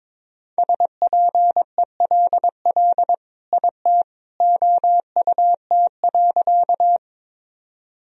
Morse code, heard as a keyed tone, about 22 wpm, 700 Hz. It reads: SPELL IT OUT.
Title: Sentences from Top 1000 Words - Review Part 1 - 22wpm